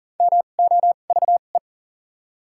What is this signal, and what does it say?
Morse code, heard as a keyed tone, about 40 wpm, spaced as for 29 wpm, 700 Hz. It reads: MOVE
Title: 2x Character Spacing - Top 200 Words - 40wpm